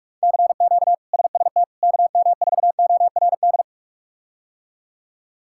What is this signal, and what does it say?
Morse code, heard as a keyed tone, about 42 wpm, 700 Hz. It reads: CQ SST KM4ORD
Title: K1USN SST QSO Practice - 45wpm